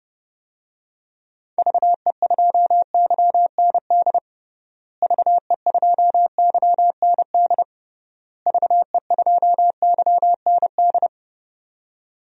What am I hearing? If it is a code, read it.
Morse code, heard as a keyed tone, about 30 wpm, 700 Hz. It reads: VE2YND VE2YND VE2YND
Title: DXCC-100 Call Signs - 6 Characters - Repeated 3x - 30wpm